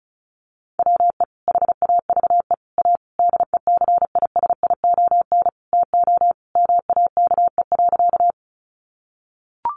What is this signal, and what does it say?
Morse code, heard as a keyed tone, about 34 wpm, 700 Hz. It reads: WE HAVE A DECISION TO MAKE.
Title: Sentences from Top 1000 Words 35wpm